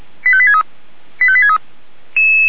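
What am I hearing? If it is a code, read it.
Five-tone selective call (ZVEI), heard as quick runs of five tones, tones of about 70 ms. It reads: 85863 85863
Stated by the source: ZVEI five-tone selective call received on an SDR.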